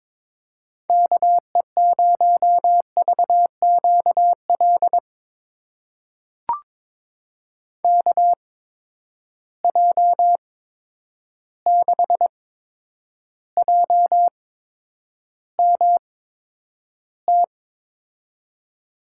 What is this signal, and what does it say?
Morse code, heard as a keyed tone, about 22 wpm, spaced as for 9 wpm, 700 Hz. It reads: KE0VQL KJ6JMT
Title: USA Call Signs - 6 Characters - 8x Character Spacing - 22wpm